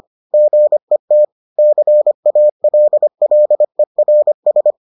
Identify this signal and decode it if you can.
Morse code, heard as a keyed tone, about 25 wpm, 600 Hz. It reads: GET CALLERS